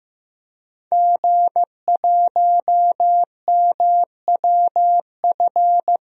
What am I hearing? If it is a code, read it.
Morse code, heard as a keyed tone, about 15 wpm, 700 Hz. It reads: G1MWF
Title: DXCC-100 Call Signs - 5 Characters - Repeated 3x - 15wpm